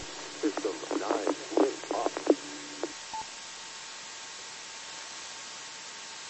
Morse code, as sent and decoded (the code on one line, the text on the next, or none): none